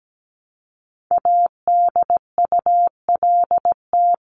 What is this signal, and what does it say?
Morse code, heard as a keyed tone, about 17 wpm, 700 Hz. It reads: ADULT
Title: Top 1501-1750 Words 17wpm